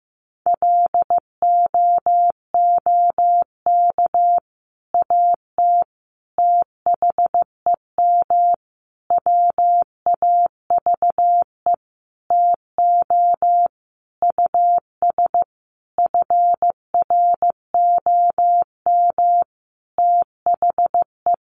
Morse code, heard as a keyed tone, about 15 wpm, 700 Hz. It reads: LOOK AT THEM WAVE TO US FROM THE
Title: Sentences from Top 1400 Words 15wpm